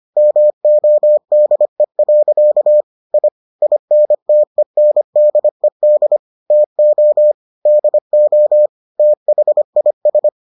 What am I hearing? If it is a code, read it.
Morse code, heard as a keyed tone, about 25 wpm, 600 Hz. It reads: MODE. I INTENDED TO DO THIS